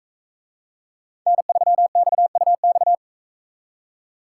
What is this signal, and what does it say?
Morse code, heard as a keyed tone, about 40 wpm, 700 Hz. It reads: N3XUX